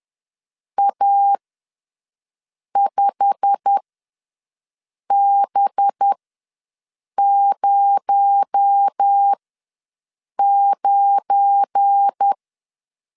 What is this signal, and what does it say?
Morse code, heard as a keyed tone, about 10 wpm, spaced as for 5 wpm, 800 Hz. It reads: A5B09
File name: LEZIONI / 9 28 ws 10wpm 4124